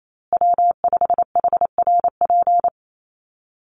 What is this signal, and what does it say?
Morse code, heard as a keyed tone, about 28 wpm, 700 Hz. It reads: W5HRP
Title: USA Call Signs Repeated 2x 28wpm